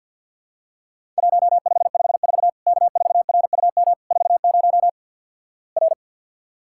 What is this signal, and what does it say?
Morse code, heard as a keyed tone, about 50 wpm, 700 Hz. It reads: 1554 K4RUM 40 R
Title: Worked All Europe DX Contest - QTC Exchange with Repeats - 50wpm